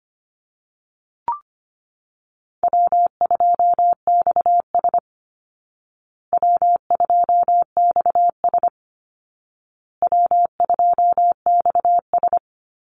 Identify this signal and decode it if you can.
Morse code, heard as a keyed tone, about 25 wpm, 700 Hz. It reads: W2XS W2XS W2XS
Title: Easy Call Signs Repeated 3x - 25wpm